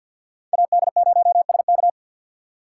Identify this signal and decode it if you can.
Morse code, heard as a keyed tone, about 51 wpm, 700 Hz. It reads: AD0SK
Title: USA Call Signs - 2x Character Spacing - 50wpm